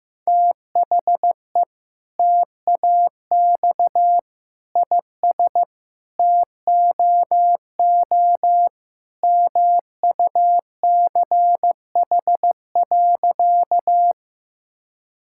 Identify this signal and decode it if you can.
Morse code, heard as a keyed tone, about 15 wpm, 700 Hz. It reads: THE TAX IS TOO MUCH.
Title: Sentences from Top 1000 Words - Review Part 5 - 15wpm